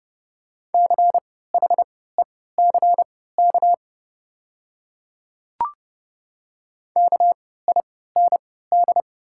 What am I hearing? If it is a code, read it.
Morse code, heard as a keyed tone, about 30 wpm, spaced as for 17 wpm, 700 Hz. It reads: CHECK KIND
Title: 3x Character Spacing - Top 500 Words - 30wpm